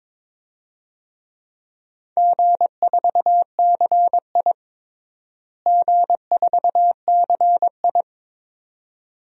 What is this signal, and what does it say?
Morse code, heard as a keyed tone, about 22 wpm, 700 Hz. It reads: G4CI G4CI
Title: DXCC-100 Call Signs - 4 Characters - Repeated 2x - 22wpm